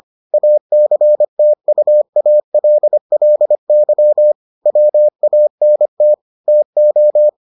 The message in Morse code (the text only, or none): ACTUALLY WANT TO